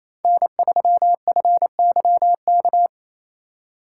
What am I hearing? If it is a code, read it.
Morse code, heard as a keyed tone, about 33 wpm, 700 Hz. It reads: N3FYK
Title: Speed Racing - USA 1x3 Call Signs 28wpm